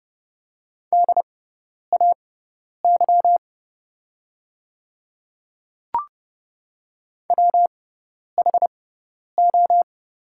Morse code, heard as a keyed tone, about 32 wpm, spaced as for 10 wpm, 690 Hz. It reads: DAY WHO
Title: 6x Character Spacing - Top 200 Words - 30wpm